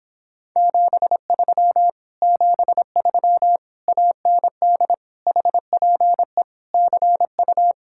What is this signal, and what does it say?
Morse code, heard as a keyed tone, about 26 wpm, 700 Hz. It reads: 73 73 AND HPE CU